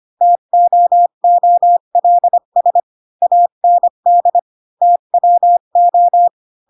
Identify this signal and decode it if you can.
Morse code, heard as a keyed tone, about 25 wpm, 700 Hz. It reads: TOOLS AND TWO
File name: Pinocchio / Pinocchio ch8 20 25 ik1hge